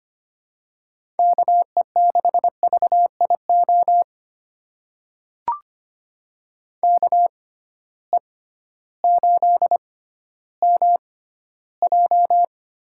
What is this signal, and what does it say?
Morse code, heard as a keyed tone, about 25 wpm, spaced as for 13 wpm, 700 Hz. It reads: KE6VIO KE8MJ
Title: USA Call Signs - 6 Characters - 6x Character Spacing - 25wpm